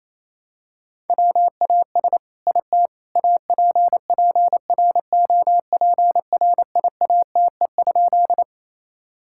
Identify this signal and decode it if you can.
Morse code, heard as a keyed tone, about 28 wpm, 700 Hz. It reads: WAS IT APPROPRIATE?
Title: Sentences from Top 1000 Words - Review Part 5 - 28wpm